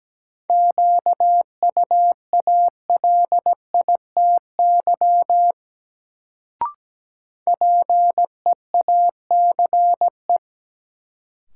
Morse code, heard as a keyed tone, about 17 wpm, 700 Hz. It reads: QUALITY PEACE
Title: Sentences from Top 1500 Words - New Word Drill Alt - 17wpm